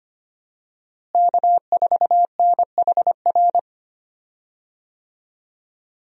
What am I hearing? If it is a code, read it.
Morse code, heard as a keyed tone, about 25 wpm, 700 Hz. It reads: K4NHR
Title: USA Call Signs - 5 Characters - Repeated 3x - 25wpm